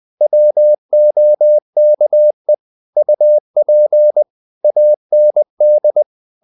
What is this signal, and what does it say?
Morse code, heard as a keyed tone, about 20 wpm, 600 Hz. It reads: WOKE UP AND
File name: the old man and the sea / the old man and the sea part 1 20wpm